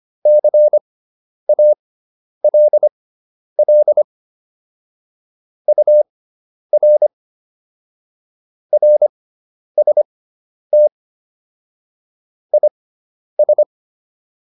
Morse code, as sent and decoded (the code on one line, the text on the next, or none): -.-. .- .-.. .-.. / ..- .-. / .-. ... - / .. ...
CALL UR RST IS